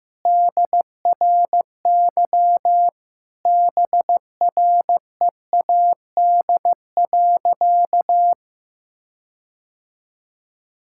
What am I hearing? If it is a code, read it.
Morse code, heard as a keyed tone, about 16 wpm, 700 Hz. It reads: DRY BREAD.
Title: Sentences from Top 1600 Words 15wpm